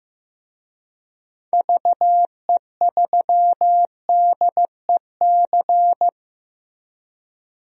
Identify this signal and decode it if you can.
Morse code, heard as a keyed tone, about 15 wpm, 700 Hz. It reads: VE3DEC
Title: DXCC-100 Call Signs - 6 Characters - Repeated 3x - 15wpm